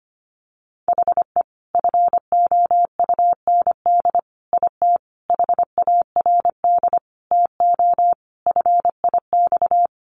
Morse code, heard as a keyed tone, about 25 wpm, 700 Hz. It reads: HE FOUND IT HARD TO FIX
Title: Sentences from Top 1400 Words 25wpm